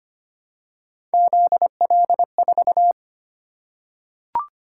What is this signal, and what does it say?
Morse code, heard as a keyed tone, about 25 wpm, 700 Hz. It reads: ZL4
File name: Three Letter-Number Combinations - 25wpm